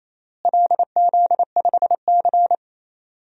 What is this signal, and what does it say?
Morse code, heard as a keyed tone, about 27 wpm, 700 Hz. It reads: LZ5C
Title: DXCC-100 Call Signs - 4 Characters - Repeated 3x - 28wpm